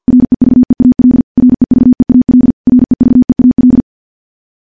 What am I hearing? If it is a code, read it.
Morse code, heard as a keyed tone, about 50 wpm, 250 Hz. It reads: REVEAL REVEAL REVEAL
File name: morse / morse-training-50WPM